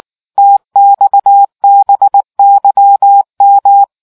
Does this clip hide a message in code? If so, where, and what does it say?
Yes: Morse code from 0.4 to 3.8 s, heard as a keyed tone, about 19 wpm, 800 Hz. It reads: TXBYM